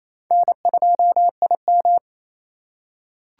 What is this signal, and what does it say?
Morse code, heard as a keyed tone, about 28 wpm, 700 Hz. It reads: N2IM
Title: USA Call Signs - 4 Characters - 4x Character Spacing - 28wpm